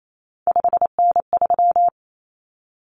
Morse code, heard as a keyed tone, about 28 wpm, 700 Hz. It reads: 5N3
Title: Common Cut Numbers - 3-Digit 28wpm